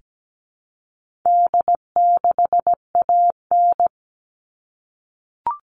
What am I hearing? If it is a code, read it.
Morse code, heard as a keyed tone, about 17 wpm, 700 Hz. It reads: D6AN
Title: All Cut Numbers - 4-Digit 17wpm